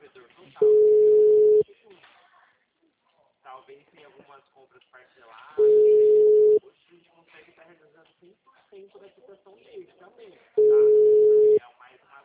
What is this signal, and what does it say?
Ringback tone (Europe, 425 Hz), with speech.